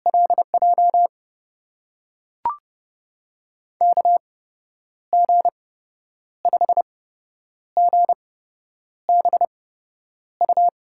Morse code, heard as a keyed tone, about 31 wpm, spaced as for 9 wpm, 700 Hz. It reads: LJ KG5GBU
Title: USA Call Signs - 6 Characters - 8x Character Spacing - 30wpm